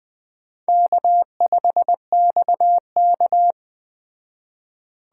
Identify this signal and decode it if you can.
Morse code, heard as a keyed tone, about 20 wpm, 700 Hz. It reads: K5XK